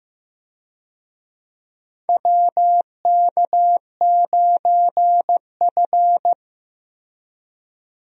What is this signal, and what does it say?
Morse code, heard as a keyed tone, about 15 wpm, 700 Hz. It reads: WK9F
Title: USA Call Signs - 4 Characters - Repeated 3x - 15wpm